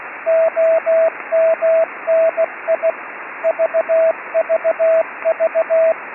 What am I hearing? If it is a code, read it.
Morse code, heard as a keyed tone, about 16 wpm, 650 Hz, with radio noise. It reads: OMNI VVV
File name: ED7YCA-B